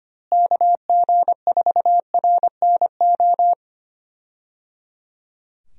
Morse code, heard as a keyed tone, about 25 wpm, 700 Hz. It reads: KG4RNO
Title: USA Call Signs - 6 Characters - Repeated 3x - 25wpm